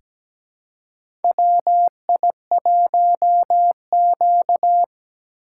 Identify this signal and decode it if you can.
Morse code, heard as a keyed tone, about 17 wpm, 700 Hz. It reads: WI1Q
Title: USA Call Signs - 4 Characters - 4x Character Spacing - 17wpm